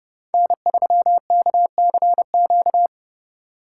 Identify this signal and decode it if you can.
Morse code, heard as a keyed tone, about 31 wpm, 700 Hz. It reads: N3KCQ
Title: USA Call Signs - 5 Characters - 6x Character Spacing - 30wpm